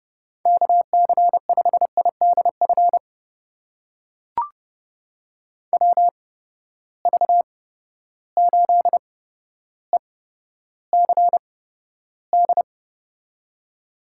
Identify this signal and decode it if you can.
Morse code, heard as a keyed tone, about 30 wpm, spaced as for 13 wpm, 700 Hz. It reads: KC5IDF WV8ECD
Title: USA Call Signs - 6 Characters - 8x Character Spacing - 30wpm